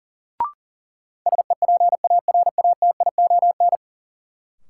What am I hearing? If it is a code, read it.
Morse code, heard as a keyed tone, about 40 wpm, 700 Hz. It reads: SEPARATION
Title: Sets of 5 Words - New Word Drill - 40wpm